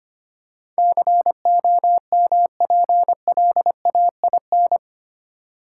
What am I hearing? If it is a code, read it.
Morse code, heard as a keyed tone, about 25 wpm, 700 Hz. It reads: COMPLAIN
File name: Eight Letter Words - 25wpm